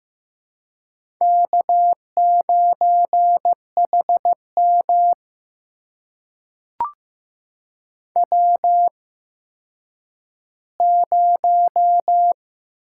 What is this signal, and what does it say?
Morse code, heard as a keyed tone, about 15 wpm, spaced as for 9 wpm, 700 Hz. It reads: K9HM W0
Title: USA Call Signs - 4 Characters - 8x Character Spacing - 15wpm